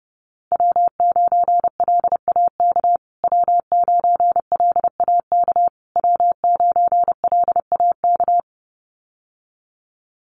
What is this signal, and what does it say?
Morse code, heard as a keyed tone, about 30 wpm, 700 Hz. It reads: W9LAK W9LAK W9LAK
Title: USA Call Signs Repeated 3x 30wpm